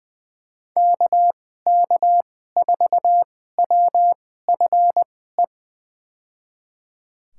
Morse code, heard as a keyed tone, about 20 wpm, spaced as for 14 wpm, 700 Hz. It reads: KK4WFE
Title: USA Call Signs - 6 Characters - 2x Character Spacing - 20wpm